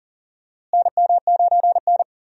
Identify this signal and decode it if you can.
Morse code, heard as a keyed tone, about 40 wpm, 700 Hz. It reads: NM9N
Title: USA Call Signs - 4 Characters - Repeated 2x - 40wpm